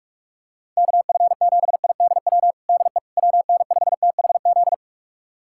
Morse code, heard as a keyed tone, about 41 wpm, 700 Hz. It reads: KF7IDW DE WN5THZ
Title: USA Callsign DE Callsign - 6 Characters - 45wpm